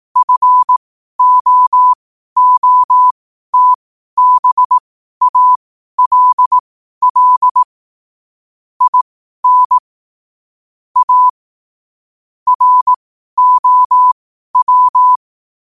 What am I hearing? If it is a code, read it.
Morse code, heard as a keyed tone, about 18 wpm, spaced as for 12 wpm, 1000 Hz. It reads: FOOTBALL IN A ROW
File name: qotdfiles / 12wpm